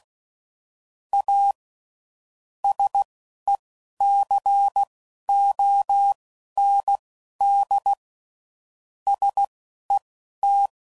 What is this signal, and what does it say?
Morse code, heard as a keyed tone, about 16 wpm, spaced as for 11 wpm, 800 Hz. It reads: A SECOND SET
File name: CW Code Groups @10WPM / Technical Subjects / SwPwrSup